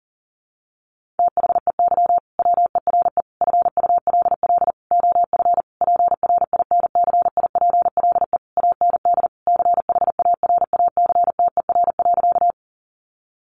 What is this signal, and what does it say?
Morse code, heard as a keyed tone, about 40 wpm, 700 Hz. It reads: THEY WERE FULL OF PRINCIPLE AND CHARACTER.
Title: Sentences from Top 1400 Words 40wpm